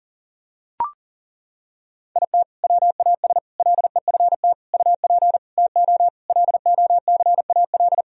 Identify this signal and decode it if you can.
Morse code, heard as a keyed tone, about 37 wpm, 700 Hz. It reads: IT WAS LEFT UP TO LOCAL
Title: Sentences from Top 1000 Words - Review Part 4 - 40wpm